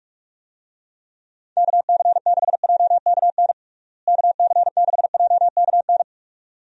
Morse code, heard as a keyed tone, about 46 wpm, 700 Hz. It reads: KC6JKN KC6JKN
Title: USA Call Signs - 6 Characters - Repeated 2x - 45wpm